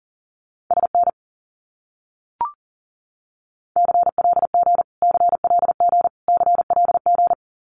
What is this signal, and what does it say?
Morse code, heard as a keyed tone, about 40 wpm, 700 Hz. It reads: SN CLG CLG CLG
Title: Abbreviations Repeated 3x - 40wpm